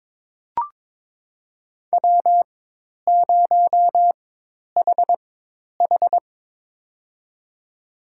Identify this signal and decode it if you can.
Morse code, heard as a keyed tone, about 22 wpm, spaced as for 10 wpm, 700 Hz. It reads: W0HH
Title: USA Call Signs - 4 Characters - 4x Character Spacing - 22wpm